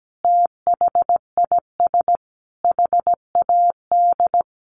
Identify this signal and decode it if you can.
Morse code, heard as a keyed tone, about 17 wpm, 700 Hz. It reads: THIS HAD
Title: Sentences from Top 1100 Words 17wpm